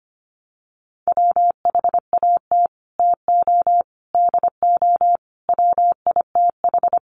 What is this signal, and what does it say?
Morse code, heard as a keyed tone, about 25 wpm, 700 Hz. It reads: WHAT TO DO WITH
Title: Sets of 5 Words Encore2 25wpm